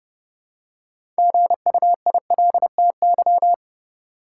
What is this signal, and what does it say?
Morse code, heard as a keyed tone, about 29 wpm, 700 Hz. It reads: GUILTY
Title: Six Letter Words - 30wpm